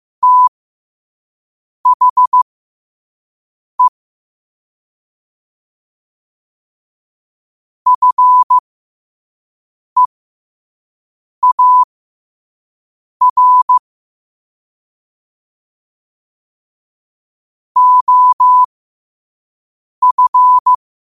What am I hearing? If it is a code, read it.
Morse code, heard as a keyed tone, about 15 wpm, spaced as for 5 wpm, 1000 Hz. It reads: THE FEAR OF
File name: qotdfiles / 5wpm